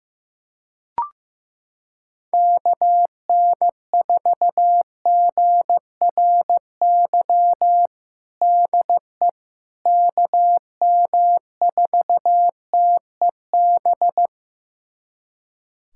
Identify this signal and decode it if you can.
Morse code, heard as a keyed tone, about 16 wpm, 700 Hz. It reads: KN4GRY DE KM4TEB
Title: USA Callsign DE Callsign 15wpm